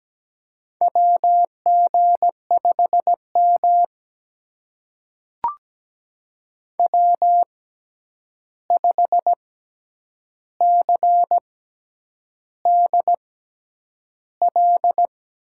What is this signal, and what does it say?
Morse code, heard as a keyed tone, about 17 wpm, spaced as for 8 wpm, 700 Hz. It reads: WG5M W5CDL
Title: USA Call Signs - 6x Character Spacing - 17wpm